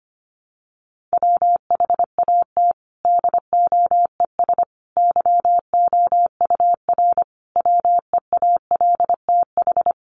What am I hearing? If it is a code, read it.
Morse code, heard as a keyed tone, about 25 wpm, 700 Hz. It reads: WHAT DOES YOUR WEALTH
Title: Sentences from Top 1400 Words 25wpm